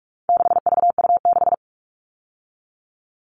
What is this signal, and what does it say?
Morse code, heard as a keyed tone, about 44 wpm, 700 Hz. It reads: 6VU6
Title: All Cut Numbers - 4-Digit 45wpm